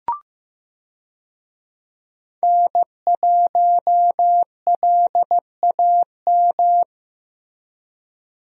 Morse code, heard as a keyed tone, about 15 wpm, 700 Hz. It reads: N1LAM
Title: USA Call Signs - 5 Characters - Repeated 3x - 15wpm